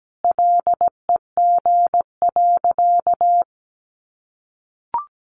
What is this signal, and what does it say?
Morse code, heard as a keyed tone, about 17 wpm, 700 Hz. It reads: LEG.
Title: Additional Easy Phrases 17wpm